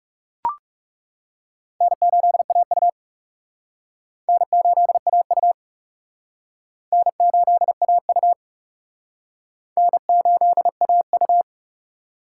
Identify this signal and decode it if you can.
Morse code, heard as a keyed tone, about 37 wpm, 700 Hz. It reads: N8AU N8AU N8AU N8AU
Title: Speed Racing - USA 1x2 Call Signs 30wpm